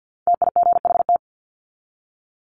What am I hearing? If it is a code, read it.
Morse code, heard as a keyed tone, about 50 wpm, 700 Hz. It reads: TIGHT
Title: Top 1751-2000 Words 50wpm